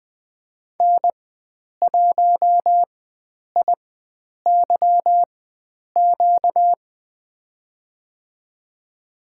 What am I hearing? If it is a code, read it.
Morse code, heard as a keyed tone, about 20 wpm, spaced as for 9 wpm, 700 Hz. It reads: N1IYQ
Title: USA Call Signs - 4x Character Spacing - 20wpm